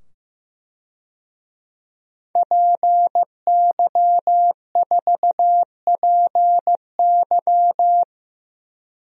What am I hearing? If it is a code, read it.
Morse code, heard as a keyed tone, about 15 wpm, 700 Hz. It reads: PY4PY
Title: DXCC-100 Call Signs - 5 Characters - 15wpm